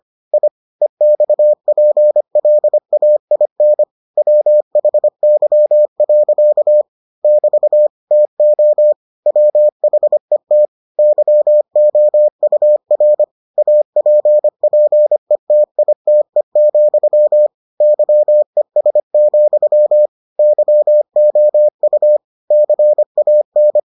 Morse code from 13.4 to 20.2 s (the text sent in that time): APPETITE, YES,